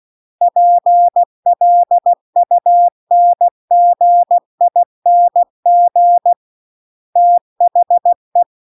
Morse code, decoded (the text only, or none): PLUNGING THE